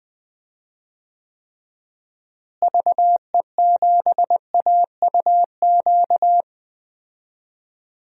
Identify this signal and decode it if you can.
Morse code, heard as a keyed tone, about 21 wpm, 700 Hz. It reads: VE7AUQ
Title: DXCC-100 Call Signs - 6 Characters - Repeated 3x - 20wpm